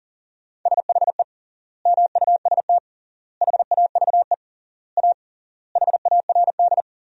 Morse code, heard as a keyed tone, about 40 wpm, 700 Hz. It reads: SHE MUST HAVE A HARD
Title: 3x Word-Spacing - Sentences from Top 300 Words - 40wpm